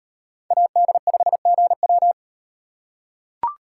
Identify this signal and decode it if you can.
Morse code, heard as a keyed tone, about 36 wpm, 700 Hz. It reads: AD5GW